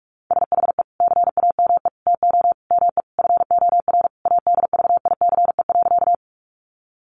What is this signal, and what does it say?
Morse code, heard as a keyed tone, about 44 wpm, 700 Hz. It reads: SHE CAME TO ME FOR ADVICE.
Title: Sentences from Top 1100 Words 45wpm